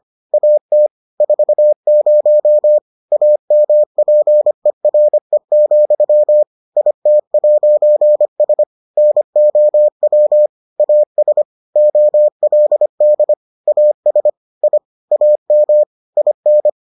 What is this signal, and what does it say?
Morse code, heard as a keyed tone, about 25 wpm, 600 Hz. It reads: AT 40 AMPERE, IT'S NOW AS OLD AS I AM IN